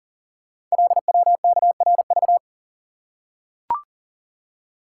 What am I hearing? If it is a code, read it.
Morse code, heard as a keyed tone, about 42 wpm, 690 Hz. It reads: LWKRV
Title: Code Groups - 5 Letters - 40wpm